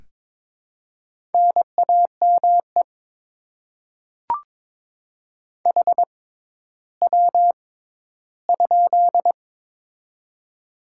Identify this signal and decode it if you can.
Morse code, heard as a keyed tone, about 22 wpm, spaced as for 13 wpm, 700 Hz. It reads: NAME HW?
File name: Top 20 QSO Elements - 6x Character Spacing - 22wpm